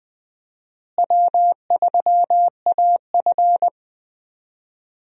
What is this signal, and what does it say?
Morse code, heard as a keyed tone, about 20 wpm, 700 Hz. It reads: W3AF
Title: USA Call Signs - 4 Characters - Repeated 2x - 20wpm